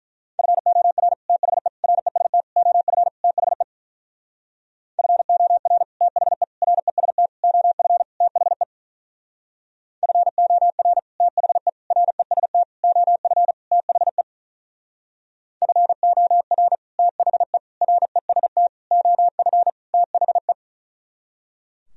Morse code, heard as a keyed tone, about 43 wpm, 700 Hz. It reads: FOR THE REST OF THE FOR THE REST OF THE FOR THE REST OF THE FOR THE REST OF THE
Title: Speed-Racing - Sets of 5 Words 35wpm